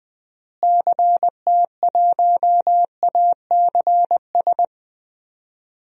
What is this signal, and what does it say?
Morse code, heard as a keyed tone, about 20 wpm, 700 Hz. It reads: CT1ACS
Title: DXCC-100 Call Signs - 6 Characters - Repeated 2x - 20wpm